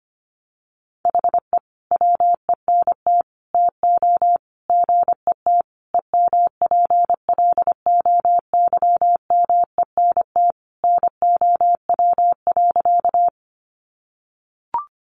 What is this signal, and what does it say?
Morse code, heard as a keyed tone, about 25 wpm, 700 Hz. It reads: HE WENT TO GET EMPLOYMENT NOW.
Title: Sentences from Top 1500 Words 25wpm